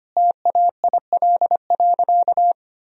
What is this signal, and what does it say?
Morse code, heard as a keyed tone, about 25 wpm, 700 Hz. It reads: TAIL.
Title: Sentences from Top 1000 Words - Review Part 3 - 25wpm